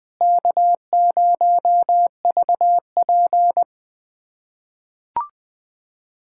Morse code, heard as a keyed tone, about 20 wpm, 700 Hz. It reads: K0VP
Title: USA Callsign DE Callsign - 4 Characters - 20wpm